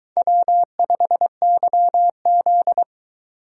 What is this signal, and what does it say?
Morse code, heard as a keyed tone, about 22 wpm, 700 Hz. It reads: W5YZ